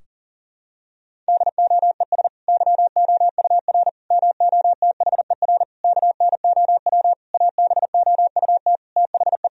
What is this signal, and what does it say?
Morse code, heard as a keyed tone, about 40 wpm, 700 Hz. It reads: DOES YOUR MOTHER KNOW ABOUT THE
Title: Sentences from Top 1000 Words - Review Part 4 - 40wpm